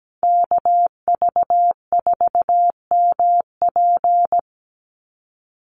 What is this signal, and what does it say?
Morse code, heard as a keyed tone, about 17 wpm, 700 Hz. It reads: KV4MP
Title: USA Call Signs 17wpm